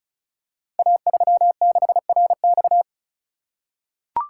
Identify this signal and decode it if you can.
Morse code, heard as a keyed tone, about 36 wpm, 700 Hz. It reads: A36RX
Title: Code Groups - 5 Letter:Numbers - 35wpm